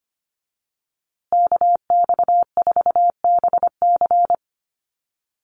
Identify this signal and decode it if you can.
Morse code, heard as a keyed tone, about 25 wpm, 700 Hz. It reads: KX4BC
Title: USA Call Signs Repeated 2x 25wpm